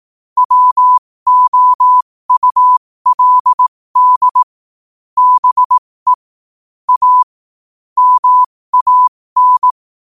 Morse code, heard as a keyed tone, about 18 wpm, spaced as for 15 wpm, 1000 Hz. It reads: WOULD BE A MAN
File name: qotdfiles / 15wpm